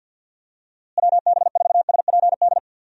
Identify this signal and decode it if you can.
Morse code, heard as a keyed tone, about 51 wpm, 700 Hz. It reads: WB4SPD
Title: USA Call Signs - 6 Characters - Repeated 2x - 50wpm